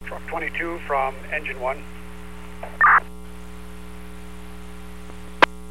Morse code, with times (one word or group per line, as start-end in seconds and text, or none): none